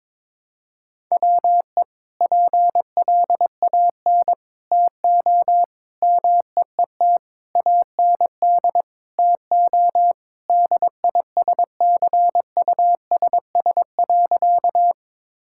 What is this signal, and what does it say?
Morse code, heard as a keyed tone, about 22 wpm, 700 Hz. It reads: WE PLAN TO MEET AND TO DISCUSS.
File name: practice / 22wpm / Sentences with 2-Word Phrases - 22wpm